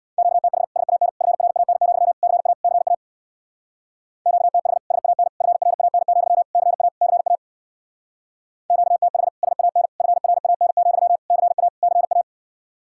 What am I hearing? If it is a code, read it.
Morse code, heard as a keyed tone, about 65 wpm, 700 Hz. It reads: QTH SAN FRAN, CA CA QTH SAN FRAN, CA CA QTH SAN FRAN, CA CA